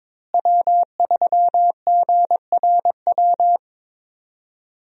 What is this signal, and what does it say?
Morse code, heard as a keyed tone, about 22 wpm, 700 Hz. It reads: W3GRW
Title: USA Call Signs - 5 Characters - Repeated 2x - 22wpm